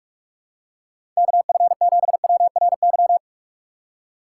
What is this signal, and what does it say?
Morse code, heard as a keyed tone, about 46 wpm, 700 Hz. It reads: KF7WRY
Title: USA Call Signs - 6 Characters - 2x Character Spacing - 45wpm